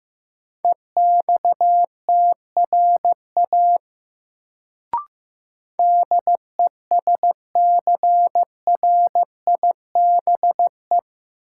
Morse code, heard as a keyed tone, about 15 wpm, 700 Hz. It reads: EXTRA DESCRIBE
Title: Sentences from Top 1000 Words - New Word Drill - 15wpm